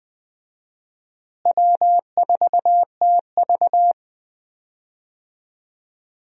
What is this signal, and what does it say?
Morse code, heard as a keyed tone, about 20 wpm, 700 Hz. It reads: W4TV
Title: USA Call Signs - 4 Characters - Repeated 2x - 20wpm